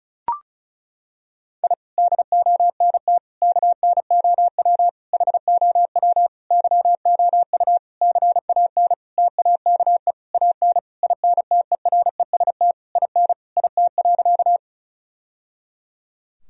Morse code, heard as a keyed tone, about 36 wpm, 700 Hz. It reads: I DONT KNOW HOW YOU CAN TAKE AN INTEREST IN IT.
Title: Sentences from Top 1000 Words - Review Part 2 - 35wpm